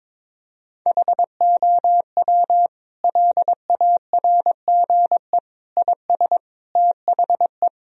Morse code, heard as a keyed tone, about 21 wpm, 700 Hz. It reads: HOW LARGE IS THE
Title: Sentences from Top 1000 Words - Review Part 4 - 22wpm